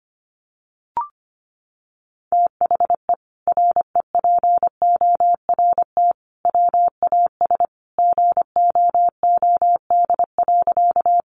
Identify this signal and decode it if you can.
Morse code, heard as a keyed tone, about 25 wpm, 700 Hz. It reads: THE REPORT WAS GOOD.
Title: Sentences from Top 900 Words 25wpm